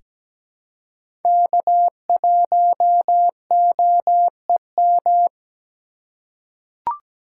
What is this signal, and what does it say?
Morse code, heard as a keyed tone, about 17 wpm, 700 Hz. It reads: K1OEM
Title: USA Call Signs - 5 Characters - 2x Character Spacing - 17wpm